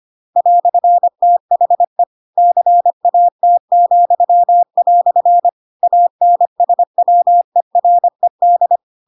Morse code, heard as a keyed tone, about 25 wpm, 700 Hz. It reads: "THE CAT," ANSWERED
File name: Pinocchio / Pinocchio ch7 25 25 ik1hge